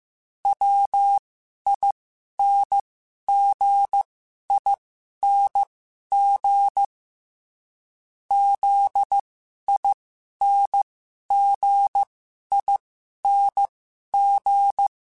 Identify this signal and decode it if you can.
Morse code, heard as a keyed tone, about 15 wpm, spaced as for 11 wpm, 800 Hz. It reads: WINGING ZINGING
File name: Special INGs / Special INGs-mp3s / Dble ings @10